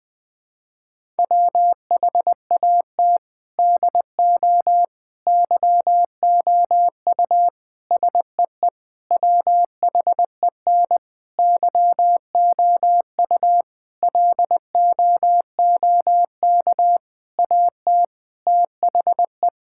Morse code, heard as a keyed tone, about 21 wpm, 700 Hz. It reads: WHAT DO YOU SEE WHEN YOU LOOK AT THE
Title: Sentences with 4-Word Phrases - Spoken Beginning - 20wpm